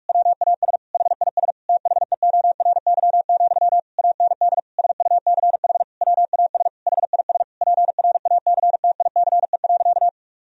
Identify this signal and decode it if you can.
Morse code, heard as a keyed tone, about 46 wpm, 700 Hz. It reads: WAS HIS THEORY, AND SUCH WAS HIS PRACTICE.
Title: Sentences from Top 1000 Words - Review Part 4 - 45wpm